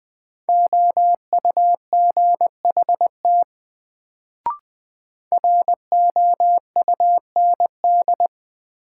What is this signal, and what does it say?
Morse code, heard as a keyed tone, about 20 wpm, 700 Hz. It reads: OUGHT ROUND
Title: Sets of 5 Words Encore - New Word Drill - 20wpm